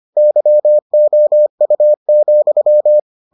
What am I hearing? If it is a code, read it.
Morse code, heard as a keyed tone, about 25 wpm, 600 Hz. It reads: YOU,